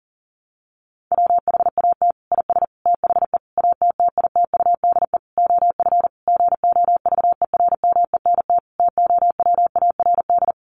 Morse code, heard as a keyed tone, about 40 wpm, 700 Hz. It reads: WHAT IS THE ATTITUDE OF GOVERMENT TOWARD
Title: Sentences from Top 1300 Words 40wpm